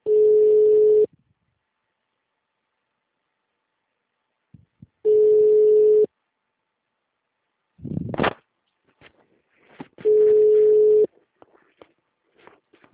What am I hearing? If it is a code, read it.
Ringback tone (Europe, 425 Hz).